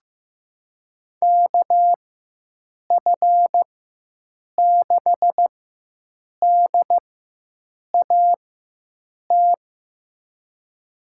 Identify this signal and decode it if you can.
Morse code, heard as a keyed tone, about 15 wpm, spaced as for 7 wpm, 700 Hz. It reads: KF6DAT
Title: USA Call Signs - 6 Characters - 4x Character Spacing - 15wpm